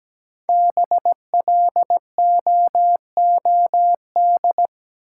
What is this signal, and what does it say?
Morse code, heard as a keyed tone, about 17 wpm, 700 Hz. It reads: BLOOD
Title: Sentences from Top 1000 Words - Review Part 2 - New Word Drill - 17wpm